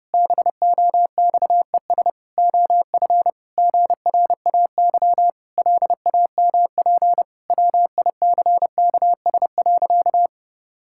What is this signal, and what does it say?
Morse code, heard as a keyed tone, about 30 wpm, 700 Hz. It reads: BOXES OF GRAY LAMP WICKS.
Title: Pangram Challenge v2 - 30wpm